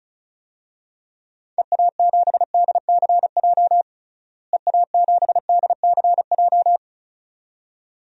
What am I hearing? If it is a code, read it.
Morse code, heard as a keyed tone, about 36 wpm, 700 Hz. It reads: EA7DCJ EA7DCJ
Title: DXCC-100 Call Signs - 6 Characters - Repeated 2x - 35wpm